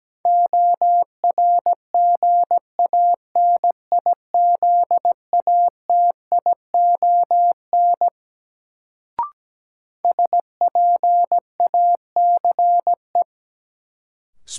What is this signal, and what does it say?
Morse code, heard as a keyed tone, about 17 wpm, 700 Hz. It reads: ORGANIZATION SPACE
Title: Sentences from Top 1000 Words - Review Part 5 - New Word Drill - 17wpm